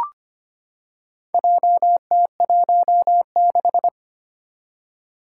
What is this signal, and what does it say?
Morse code, heard as a keyed tone, about 25 wpm, 700 Hz. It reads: JT16
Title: Four Challenging Letter-Number Combinations 25wpm